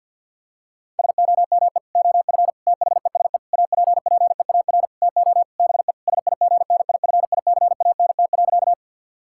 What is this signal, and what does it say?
Morse code, heard as a keyed tone, about 51 wpm, 700 Hz. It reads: SOME OF THESE APPEAR TO BE SIGNIFICANT.
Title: Sentences from Top 1000 Words - Review Part 6 - 50wpm